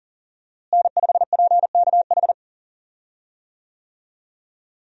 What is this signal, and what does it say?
Morse code, heard as a keyed tone, about 40 wpm, 700 Hz. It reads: N5PKH
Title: USA Call Signs - 5 Characters - Repeated 2x - 40wpm